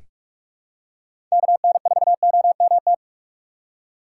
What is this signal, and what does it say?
Morse code, heard as a keyed tone, about 46 wpm, 700 Hz. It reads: KN4OMT
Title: USA Call Signs - 6 Characters - 8x Character Spacing - 45wpm